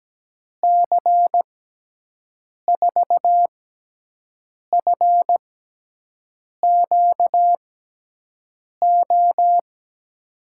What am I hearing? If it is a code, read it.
Morse code, heard as a keyed tone, about 17 wpm, spaced as for 6 wpm, 700 Hz. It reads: C4FQO
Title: USA Call Signs - 6x Character Spacing - 17wpm